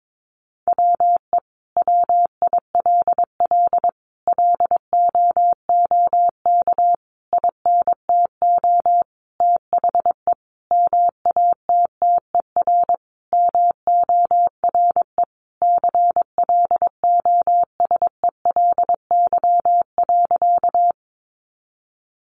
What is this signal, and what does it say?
Morse code, heard as a keyed tone, about 22 wpm, 700 Hz. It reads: WE WILL LOOK INTO THE MATTER MORE CLOSELY.
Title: Sentences from Top 1400 Words 22wpm